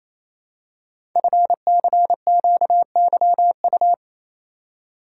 Morse code, heard as a keyed tone, about 27 wpm, 700 Hz. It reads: FCQYU
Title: Code Groups - 5 Letters - 28wpm